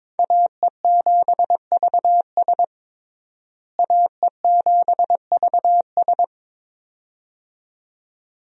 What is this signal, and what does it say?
Morse code, heard as a keyed tone, about 22 wpm, 700 Hz. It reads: AE7VS AE7VS
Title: USA Call Signs - 5 Characters - Repeated 2x - 22wpm